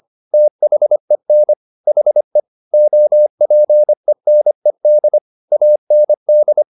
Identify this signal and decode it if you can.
Morse code, heard as a keyed tone, about 25 wpm, 600 Hz. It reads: THEN HE OPENED AND